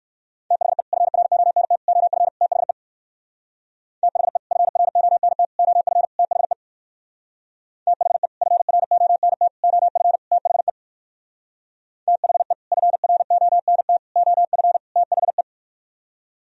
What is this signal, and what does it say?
Morse code, heard as a keyed tone, about 58 wpm, 700 Hz. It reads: THE FRONT OF THE THE FRONT OF THE THE FRONT OF THE THE FRONT OF THE